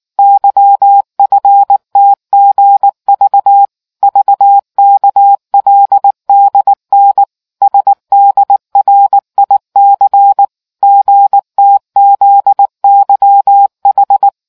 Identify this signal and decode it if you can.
Morse code, heard as a keyed tone, about 19 wpm, 800 Hz. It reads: YFTGV VKLDN SDRIC GTZYH